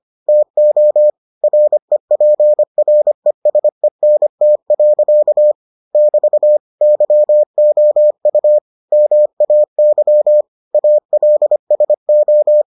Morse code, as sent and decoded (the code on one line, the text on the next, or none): - --- / .-. .--. .-. ... -. - .-.-.- / -...- / -.-- --- ..- / -- .- -.-- / .- .-.. ... ---
TO REPRESENT. <BT> YOU MAY ALSO